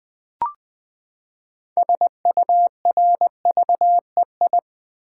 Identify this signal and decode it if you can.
Morse code, heard as a keyed tone, about 20 wpm, 700 Hz. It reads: SURVEI
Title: Build A Word - 6 Letters At A Time - 12-Letter Words - 20wpm